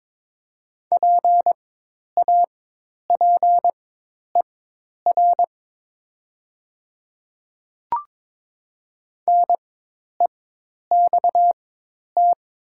Morse code, heard as a keyed tone, about 22 wpm, spaced as for 10 wpm, 700 Hz. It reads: PAPER NEXT
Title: 4x Character Spacing - Top 300 Words - 22wpm